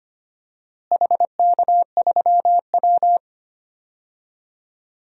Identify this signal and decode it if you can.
Morse code, heard as a keyed tone, about 25 wpm, 700 Hz. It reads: HK3W
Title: DXCC-100 Call Signs - 4 Characters - Repeated 3x - 25wpm